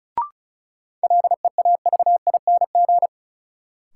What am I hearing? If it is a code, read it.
Morse code, heard as a keyed tone, about 36 wpm, 700 Hz. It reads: LEAVING
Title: Sentences from Top 1000 Words - Review Part 4 - New Word Drill - 35wpm